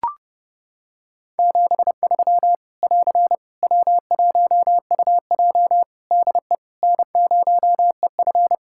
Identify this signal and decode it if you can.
Morse code, heard as a keyed tone, about 30 wpm, 700 Hz. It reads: 73 <AR> W1UJ DE N0EF
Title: Practice QSO - Exchange 4 - Ending - 30wpm